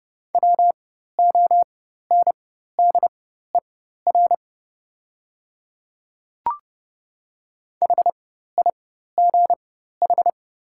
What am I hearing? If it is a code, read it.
Morse code, heard as a keyed tone, about 30 wpm, spaced as for 14 wpm, 700 Hz. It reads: WONDER HIGH
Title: 4x Character Spacing - Top 500 Words - 30wpm